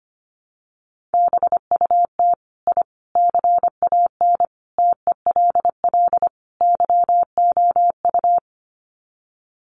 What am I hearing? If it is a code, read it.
Morse code, heard as a keyed tone, about 25 wpm, 700 Hz. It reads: BUT I CAN TELL YOU
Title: Sets of 5 Words Encore 25wpm